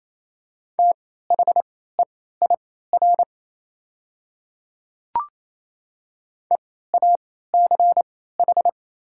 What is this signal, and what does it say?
Morse code, heard as a keyed tone, about 28 wpm, spaced as for 16 wpm, 700 Hz. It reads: THEIR EACH
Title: 3x Character Spacing - Top 100 Words - 28wpm